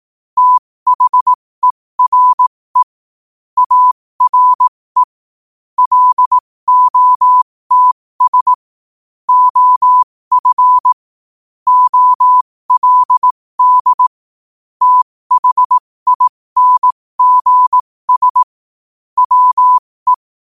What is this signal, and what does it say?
Morse code, heard as a keyed tone, about 18 wpm, spaced as for 15 wpm, 1000 Hz. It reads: THERE ARE LOTS OF OLD THINGS WE
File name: qotdfiles / 15wpm